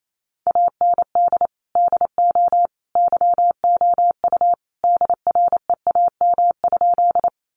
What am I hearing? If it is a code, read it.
Morse code, heard as a keyed tone, about 28 wpm, 700 Hz. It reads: AND DO YOU DREAM?
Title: Sentences from Top 1500 Words 28wpm